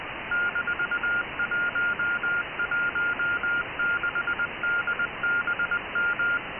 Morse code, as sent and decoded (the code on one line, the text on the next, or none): -....- .---- .---- -.... -.. -... --
-116DBM